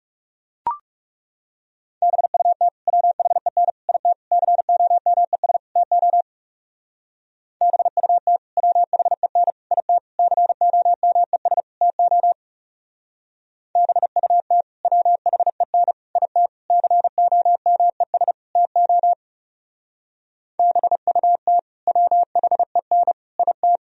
Morse code, heard as a keyed tone, about 35 wpm, 700 Hz. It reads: BUT WHEN IT COMES TO BUT WHEN IT COMES TO BUT WHEN IT COMES TO BUT WHEN IT
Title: Speed-Racing - Sets of 5 Words 30wpm